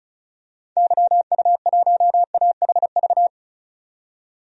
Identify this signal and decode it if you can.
Morse code, heard as a keyed tone, about 35 wpm, 700 Hz. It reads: YU1AHV
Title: DXCC-100 Call Signs - 6 Characters - 35wpm